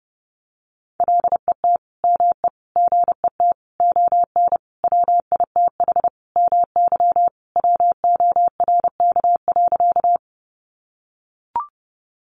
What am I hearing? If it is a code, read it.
Morse code, heard as a keyed tone, about 30 wpm, 700 Hz. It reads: LET ME GET ON WITH MY WORK.
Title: Sentences from Top 300 Words 30wpm